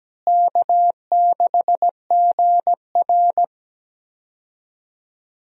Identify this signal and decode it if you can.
Morse code, heard as a keyed tone, about 17 wpm, 700 Hz. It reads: K6GR
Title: USA Call Signs - 4 Characters - Repeated 3x - 17wpm